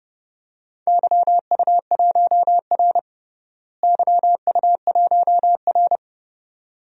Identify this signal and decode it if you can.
Morse code, heard as a keyed tone, about 32 wpm, 700 Hz. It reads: YU1R YU1R
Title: DXCC-100 Call Signs - 4 Characters - Repeated 2x - 30wpm